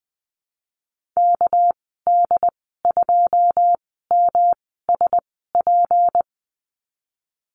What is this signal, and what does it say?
Morse code, heard as a keyed tone, about 20 wpm, spaced as for 15 wpm, 700 Hz. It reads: KD2MSP
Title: 207 Lesson 33g - Letter j - Call Signs 20wpm